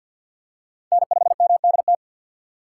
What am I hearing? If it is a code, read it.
Morse code, heard as a keyed tone, about 49 wpm, 690 Hz. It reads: N5MDT
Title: USA Call Signs - 2x Character Spacing - 50wpm